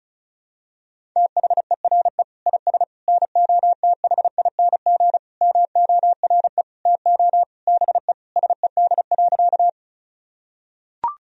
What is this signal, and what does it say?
Morse code, heard as a keyed tone, about 36 wpm, 700 Hz. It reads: THERE IS NOTHING MORE TO BE SED.
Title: Sentences from Top 500 Words and Abbreviations 35wpm